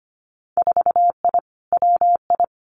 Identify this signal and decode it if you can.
Morse code, heard as a keyed tone, about 25 wpm, 700 Hz. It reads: 4I WI
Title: Winter Field Day Exchanges 25wpm